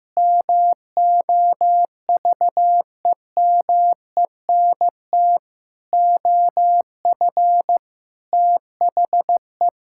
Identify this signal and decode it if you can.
Morse code, heard as a keyed tone, about 15 wpm, 700 Hz. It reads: MOVEMENT OF THE
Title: Pangram Challenge v2 - 15wpm